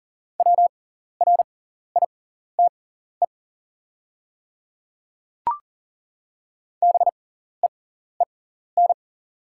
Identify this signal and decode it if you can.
Morse code, heard as a keyed tone, about 40 wpm, spaced as for 14 wpm, 690 Hz. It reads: WRITE BEEN
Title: 6x Character Spacing - Top 100 Words - 40wpm